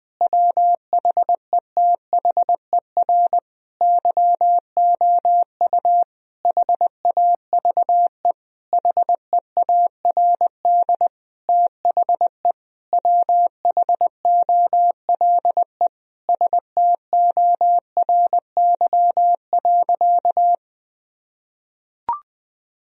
Morse code, heard as a keyed tone, about 20 wpm, 700 Hz. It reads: WHETHER YOU HAVE HEARD THE WHOLE STORY.
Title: Sentences from Top 1000 Words - Review Part 3 - 20wpm